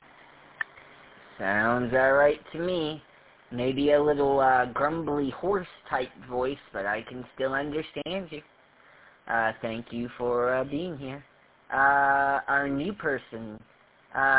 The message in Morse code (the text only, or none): none